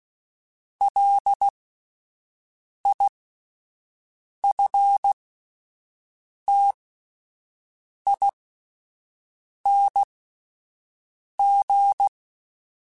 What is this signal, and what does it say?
Morse code, heard as a keyed tone, about 16 wpm, spaced as for 5 wpm, 800 Hz. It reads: LIFTING
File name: Alphabetical / Alphabetical mp3s / Ls @5